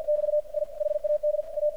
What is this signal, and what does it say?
Morse code, heard as a keyed tone, about 51 wpm, 600 Hz, with radio noise. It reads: <BT> I HAD A
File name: archived / CRSnet